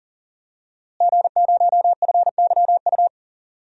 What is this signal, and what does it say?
Morse code, heard as a keyed tone, about 40 wpm, 700 Hz. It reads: G0FYU
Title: DXCC-100 Call Signs - 5 Characters - Repeated 2x - 40wpm